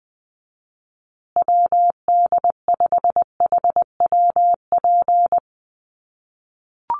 Morse code, heard as a keyed tone, about 20 wpm, 700 Hz. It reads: WD5HWP
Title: USA Call Signs Repeated 2x 20wpm